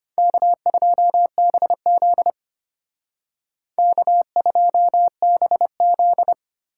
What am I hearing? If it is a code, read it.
Morse code, heard as a keyed tone, about 30 wpm, 700 Hz. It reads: K2BZ K2BZ
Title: Speed Racing - USA Call Signs 25wpm